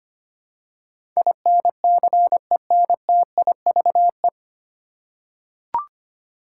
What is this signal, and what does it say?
Morse code, heard as a keyed tone, about 25 wpm, 700 Hz. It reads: INCENTIVE
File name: Nine Letter Words - 25wpm